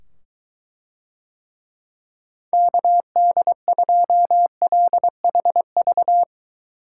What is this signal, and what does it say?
Morse code, heard as a keyed tone, about 21 wpm, 700 Hz. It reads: KD2LHV